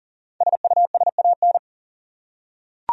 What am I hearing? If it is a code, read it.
Morse code, heard as a keyed tone, about 38 wpm, 700 Hz. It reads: SUSAN
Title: USA Common Names - Max 5 Letters - 40wpm